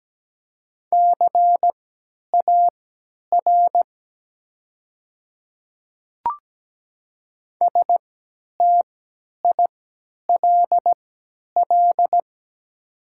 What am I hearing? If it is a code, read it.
Morse code, heard as a keyed tone, about 17 wpm, spaced as for 10 wpm, 700 Hz. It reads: CAR STILL
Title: 3x Character Spacing - Top 300 Words - 17wpm